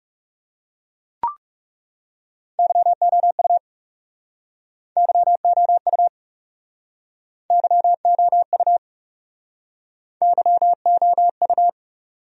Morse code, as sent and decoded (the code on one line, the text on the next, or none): -.-- --- ..- / -.-- --- ..- / -.-- --- ..- / -.-- --- ..-
YOU YOU YOU YOU